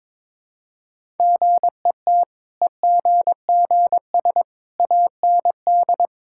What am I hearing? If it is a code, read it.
Morse code, heard as a keyed tone, about 22 wpm, 690 Hz. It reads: GET EGGS AND
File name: practice / 22wpm / Easy Sentences with Hints - 22wpm